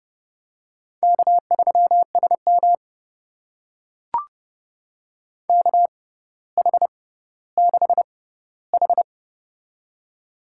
Morse code, heard as a keyed tone, about 30 wpm, spaced as for 15 wpm, 700 Hz. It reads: K3SM KH6H
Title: USA Call Signs - 4 Characters - 6x Character Spacing - 30wpm